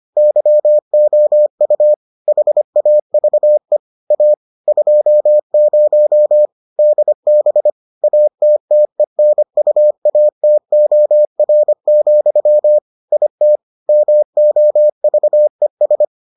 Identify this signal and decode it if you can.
Morse code, heard as a keyed tone, about 25 wpm, 600 Hz. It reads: YOU HAVE A 20 DB ATTENUATOR, IT MOVES